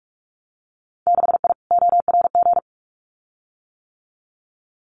Morse code, heard as a keyed tone, about 44 wpm, 700 Hz. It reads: 6I ORG
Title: Winter Field Day Exchanges Encore 45wpm